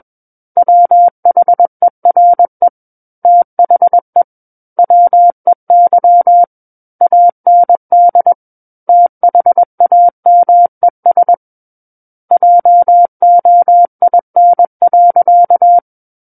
Morse code, heard as a keyed tone, about 21 wpm, 700 Hz. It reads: WHERE THE WEY AND THAMES JOIN.
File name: Book One / 12-Chapter 12 - 21 WPM